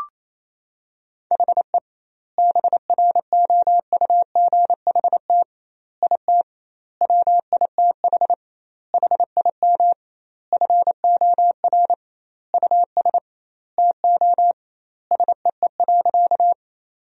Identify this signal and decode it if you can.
Morse code, heard as a keyed tone, about 28 wpm, 700 Hz. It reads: HE BROUGHT IT WITH HIM FOR US TO SEE.
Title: 2x Word-Spacing - Sentences from Top 500 Words - 28wpm